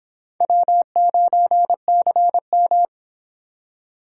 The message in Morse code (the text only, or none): W9CM